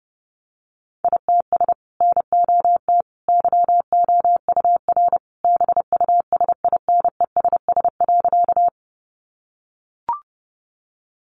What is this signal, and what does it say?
Morse code, heard as a keyed tone, about 30 wpm, 700 Hz. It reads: ITS NOT YOUR BUSINESS.
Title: Sentences from Top 600 Words 30wpm